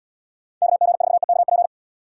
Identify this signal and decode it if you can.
Morse code, heard as a keyed tone, about 68 wpm, 700 Hz. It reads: KG4EKP